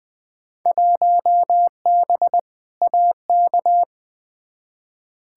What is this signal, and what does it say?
Morse code, heard as a keyed tone, about 20 wpm, 700 Hz. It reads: 1B AK
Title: Field Day Exchanges Encore - 20wpm